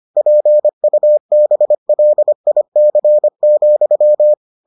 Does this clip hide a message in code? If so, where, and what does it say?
Yes: Morse code from 0.2 to 4.3 s, heard as a keyed tone, about 25 wpm, 600 Hz. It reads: PUBLIC,